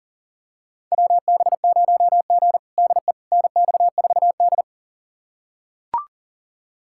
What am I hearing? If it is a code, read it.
Morse code, heard as a keyed tone, about 40 wpm, 700 Hz. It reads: WB0G DE NX4D
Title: USA Callsign DE Callsign - 4 Characters - 40wpm